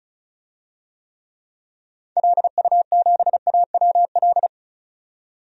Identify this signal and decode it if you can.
Morse code, heard as a keyed tone, about 36 wpm, 700 Hz. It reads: LU7AWL
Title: DXCC-100 Call Signs - 6 Characters - 35wpm